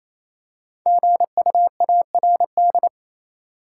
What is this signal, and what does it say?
Morse code, heard as a keyed tone, about 28 wpm, 700 Hz. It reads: GUARD
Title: Five Letter Words - 28wpm